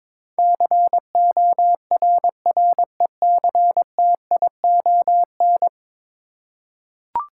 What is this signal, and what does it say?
Morse code, heard as a keyed tone, about 22 wpm, 700 Hz. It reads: CORRECTION
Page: morse-code-ninja-practice-22wpm